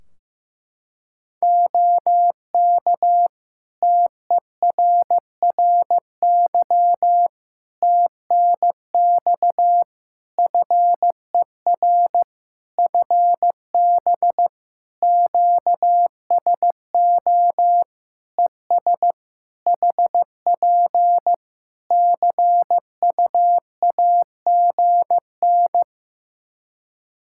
Morse code, heard as a keyed tone, about 15 wpm, 700 Hz. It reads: OK TERRY TNX FER FB QSO ES HP CUAGN
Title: Practice QSO - Exchange 4 - Ending - 15wpm